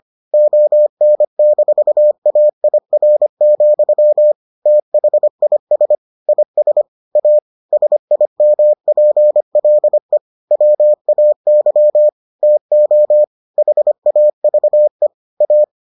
Morse code, heard as a keyed tone, about 25 wpm, 600 Hz. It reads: ON-AIR, THIS IS A SIMPLE WAY TO HAVE A